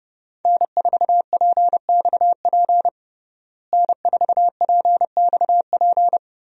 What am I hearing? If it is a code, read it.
Morse code, heard as a keyed tone, about 30 wpm, 700 Hz. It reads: N4PXP N4PXP
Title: USA Call Signs - 5 Characters - Repeated 3x - 30wpm